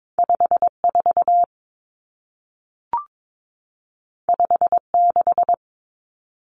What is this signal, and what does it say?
Morse code, heard as a keyed tone, about 22 wpm, 700 Hz. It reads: 54 56
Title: Common Cut Numbers - 2-Digit 22wpm